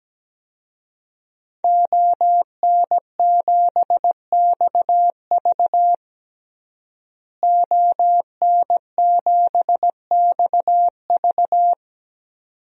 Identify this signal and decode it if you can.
Morse code, heard as a keyed tone, about 17 wpm, 700 Hz. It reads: ON7XV ON7XV
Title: DXCC-100 Call Signs - 5 Characters - Repeated 2x - 17wpm